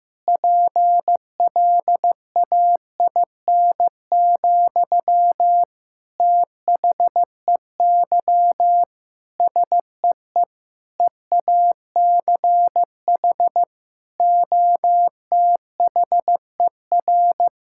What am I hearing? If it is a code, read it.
Morse code, heard as a keyed tone, about 15 wpm, 700 Hz. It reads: PLAIN, THEY SEE EACH OTHER
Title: Sentences from Top 1000 Words - Review Part 5 - 15wpm